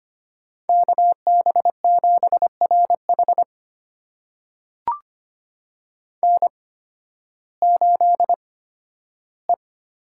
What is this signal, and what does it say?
Morse code, heard as a keyed tone, about 25 wpm, spaced as for 13 wpm, 700 Hz. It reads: KB7RH N8E
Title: USA Call Signs - 8x Character Spacing - 25wpm